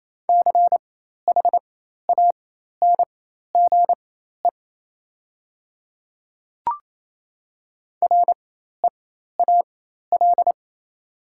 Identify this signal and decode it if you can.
Morse code, heard as a keyed tone, about 28 wpm, spaced as for 13 wpm, 690 Hz. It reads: CHANGE REAL
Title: 4x Character Spacing - Top 500 Words - 28wpm